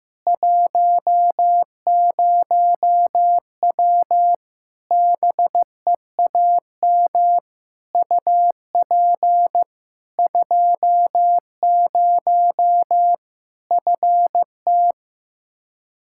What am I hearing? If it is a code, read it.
Morse code, heard as a keyed tone, about 15 wpm, 700 Hz. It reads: 10W BEAM UP 20 FT
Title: Practice QSO - Exchange 2 - Info - 15wpm